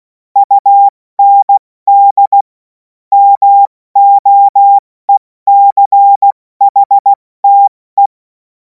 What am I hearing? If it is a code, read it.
Morse code, heard as a keyed tone, about 16 wpm, 800 Hz. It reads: UND MOECHTE
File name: Morsekurs / Witze / Witz 16. Juni 2026 70 F80